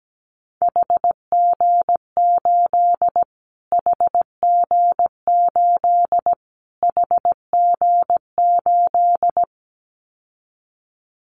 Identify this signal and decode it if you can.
Morse code, heard as a keyed tone, about 17 wpm, 700 Hz. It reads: HG8 HG8 HG8
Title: Callsign Prefixes for Top 25 DXCC - Repeated 3x - 17wpm